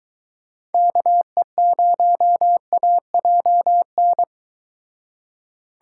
Morse code, heard as a keyed tone, about 22 wpm, 700 Hz. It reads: KE0AJN